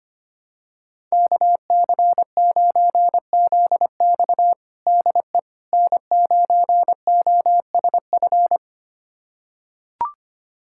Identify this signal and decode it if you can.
Morse code, heard as a keyed tone, about 25 wpm, 700 Hz. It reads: KC9ZX DE N9OSF
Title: USA Callsign DE Callsign - 5 Characters - 25wpm